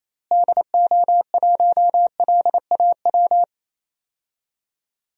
Morse code, heard as a keyed tone, about 28 wpm, 700 Hz. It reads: DO1LAW
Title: DXCC-100 Call Signs - 6 Characters - Repeated 2x - 28wpm